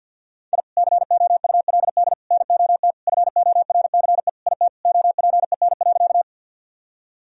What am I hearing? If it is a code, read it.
Morse code, heard as a keyed tone, about 50 wpm, 700 Hz. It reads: I COULD NOT FORCE IT OPEN.
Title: Sentences from Top 1000 Words - Review Part 3 - 50wpm